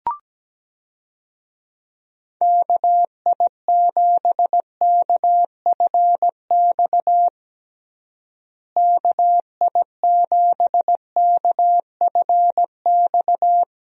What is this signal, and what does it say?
Morse code, heard as a keyed tone, about 17 wpm, 700 Hz. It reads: KI7KFX KI7KFX
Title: USA Call Signs - 6 Characters - Repeated 3x - 17wpm